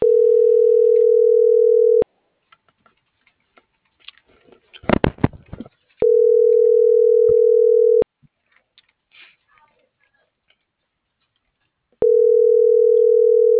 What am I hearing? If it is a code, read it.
Ringback tone (North America), 440 + 480 Hz.